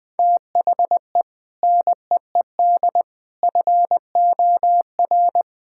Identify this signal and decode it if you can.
Morse code, heard as a keyed tone, about 21 wpm, 700 Hz. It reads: THE NEED FOR
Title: Sentences with 3-Word Phrases - 20wpm